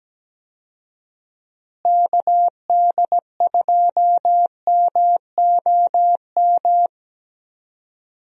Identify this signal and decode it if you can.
Morse code, heard as a keyed tone, about 17 wpm, 700 Hz. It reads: KD2MOM
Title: USA Call Signs - 6 Characters - Repeated 3x - 17wpm